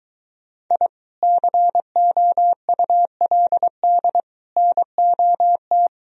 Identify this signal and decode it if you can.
Morse code, heard as a keyed tone, about 23 wpm, 700 Hz. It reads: I COULD NOT